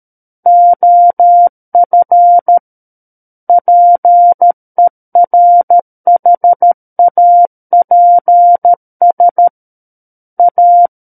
Morse code, heard as a keyed tone, about 13 wpm, 700 Hz. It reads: OF PERHAPS A
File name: Book One / 04-Chapter 4 - 13 WPM